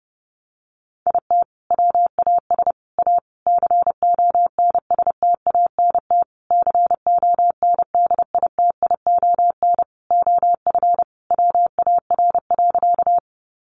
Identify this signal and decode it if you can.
Morse code, heard as a keyed tone, about 30 wpm, 700 Hz. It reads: IT WAS A CONSTANT CONDITION OF WAR.
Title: Sentences from Top 1500 Words 30wpm